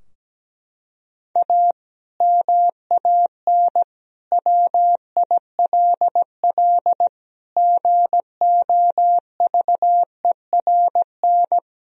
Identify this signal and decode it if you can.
Morse code, heard as a keyed tone, about 17 wpm, 700 Hz. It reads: A MAN WILL GOVERN
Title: Sentences from Top 1000 Words - Review Part 2 - 17wpm